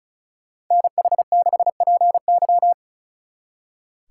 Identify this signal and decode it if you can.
Morse code, heard as a keyed tone, about 45 wpm, 700 Hz. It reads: NH6PY